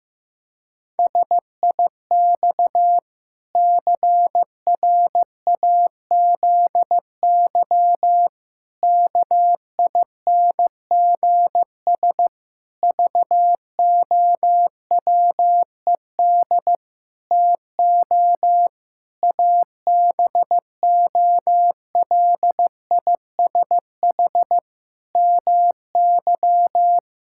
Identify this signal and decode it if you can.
Morse code, heard as a keyed tone, about 15 wpm, 700 Hz. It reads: SIX CRAZY KINGS VOWED TO ABOLISH MY
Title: Pangram Challenge v2 - 15wpm